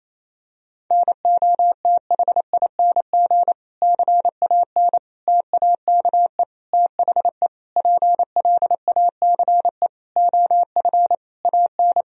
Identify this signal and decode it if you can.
Morse code, heard as a keyed tone, about 28 wpm, 700 Hz. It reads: NOTHING CAN TAKE THE PLACE OF AN
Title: Sentences from Top 1000 Words - Review Part 4 - 28wpm